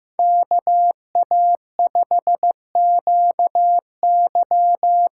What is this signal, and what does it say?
Morse code, heard as a keyed tone, about 15 wpm, 700 Hz. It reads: KA5QY
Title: USA Call Signs - 6 Characters - 6x Character Spacing - 15wpm